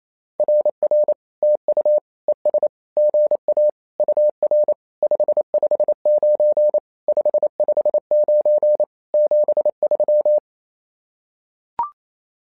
Morse code, heard as a keyed tone, about 28 wpm, 600 Hz. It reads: RR TU ES GA UR 559 559 73
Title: SOTA QSO Practice 28wpm